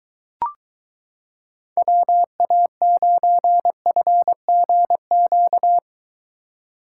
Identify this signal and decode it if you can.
Morse code, heard as a keyed tone, about 23 wpm, 700 Hz. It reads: WA9FGQ